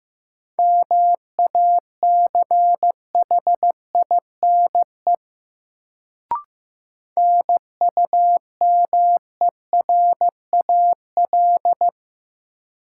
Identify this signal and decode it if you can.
Morse code, heard as a keyed tone, about 15 wpm, 700 Hz. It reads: MACHINE NUMERAL
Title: Sentences from Top 1000 Words - Review Part 2 - New Word Drill - 15wpm